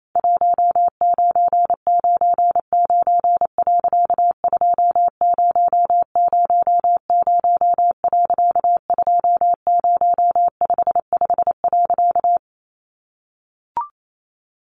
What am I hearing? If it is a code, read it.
Morse code, heard as a keyed tone, about 28 wpm, 700 Hz. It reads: 1999. 2000. 2055.
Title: A Sound Of Thunder 28wpm